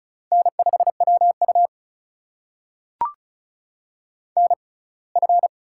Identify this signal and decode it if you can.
Morse code, heard as a keyed tone, about 34 wpm, spaced as for 23 wpm, 700 Hz. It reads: N5WU NF
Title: USA Call Signs - 4 Characters - 6x Character Spacing - 35wpm